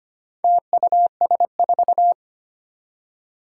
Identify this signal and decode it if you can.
Morse code, heard as a keyed tone, about 25 wpm, 700 Hz. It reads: TUS4
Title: Four Challenging Letter-Number Combinations 25wpm